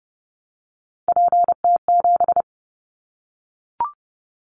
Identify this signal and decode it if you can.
Morse code, heard as a keyed tone, about 30 wpm, 700 Hz. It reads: PT7
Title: Callsign Prefixes for Top 50 DXCC - 30wpm